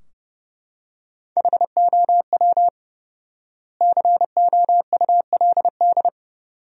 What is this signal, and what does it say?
Morse code, heard as a keyed tone, about 31 wpm, 700 Hz. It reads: HOW COULD
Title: 4x Word-Spacing - Sentences from Top 300 Words - 30wpm